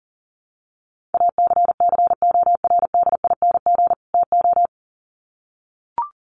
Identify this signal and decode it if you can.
Morse code, heard as a keyed tone, about 40 wpm, 700 Hz. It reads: ACCORDING TO
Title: Sets of 2 Words 40wpm